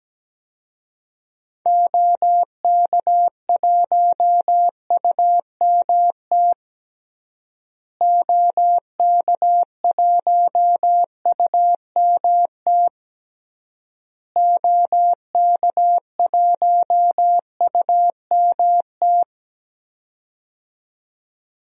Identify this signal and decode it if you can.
Morse code, heard as a keyed tone, about 17 wpm, 700 Hz. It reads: OK1UMT OK1UMT OK1UMT
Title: DXCC-100 Call Signs - 6 Characters - Repeated 3x - 17wpm